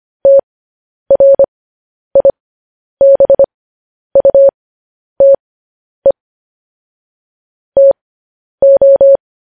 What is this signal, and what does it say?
Morse code, heard as a keyed tone, about 25 wpm, spaced as for 10 wpm, 550 Hz. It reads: TRIBUTE TO
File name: podcast / News headlines-1x-25x10